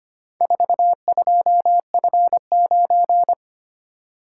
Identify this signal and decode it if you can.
Morse code, heard as a keyed tone, about 25 wpm, 700 Hz. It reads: 42F9
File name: practice / 25wpm / Four Letter-Number Combinations - 25wpm